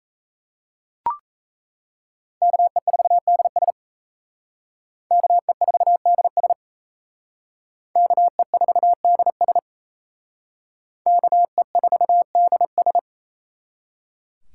Morse code, heard as a keyed tone, about 35 wpm, 700 Hz. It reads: KE4DS KE4DS KE4DS KE4DS